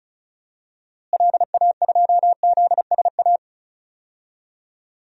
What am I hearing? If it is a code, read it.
Morse code, heard as a keyed tone, about 36 wpm, 700 Hz. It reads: LA2ZSA
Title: DXCC-100 Call Signs - 6 Characters - Repeated 2x - 35wpm